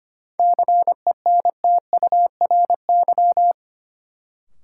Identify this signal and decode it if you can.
Morse code, heard as a keyed tone, about 25 wpm, 700 Hz. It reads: CENTURY
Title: Sets of 5 Words - New Word Drill - 25wpm